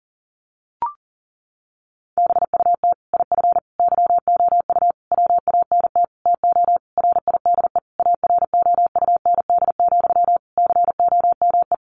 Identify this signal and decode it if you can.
Morse code, heard as a keyed tone, about 37 wpm, 700 Hz. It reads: BUT IF YOU WANT TO RIDE AROUND, COME
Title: Sentences from Top 1200 Words 40wpm